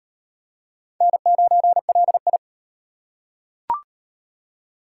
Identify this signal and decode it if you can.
Morse code, heard as a keyed tone, about 34 wpm, 700 Hz. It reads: N9LI